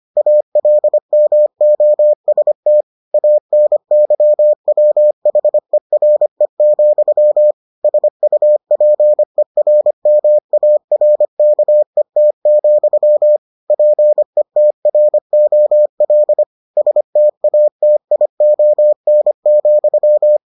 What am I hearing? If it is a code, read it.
Morse code, heard as a keyed tone, about 25 wpm, 600 Hz. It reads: ALMOST ANYWHERE, SUPERMARKET, PETROL STATION,